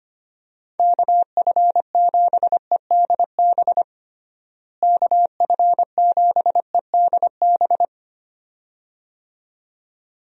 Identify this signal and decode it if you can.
Morse code, heard as a keyed tone, about 25 wpm, 700 Hz. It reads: KF7EDB KF7EDB
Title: USA Call Signs - 6 Characters - Repeated 2x - 25wpm